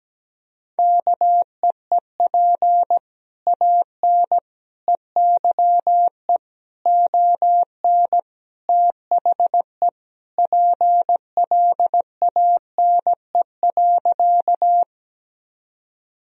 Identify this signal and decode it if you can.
Morse code, heard as a keyed tone, about 17 wpm, 700 Hz. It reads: KEEP AN EYE ON THE PLANE.
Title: Sentences from Top 1000 Words - Review Part 3 - 17wpm